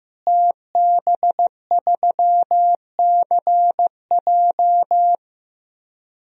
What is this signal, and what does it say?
Morse code, heard as a keyed tone, about 15 wpm, 700 Hz. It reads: TB3CJ
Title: DXCC-100 Call Signs - 5 Characters - Repeated 2x - 15wpm